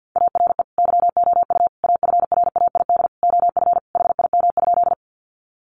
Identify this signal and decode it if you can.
Morse code, heard as a keyed tone, about 50 wpm, 700 Hz. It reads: ARE YOU AFRAID OF HIM?
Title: Sentences from Top 1500 Words 50wpm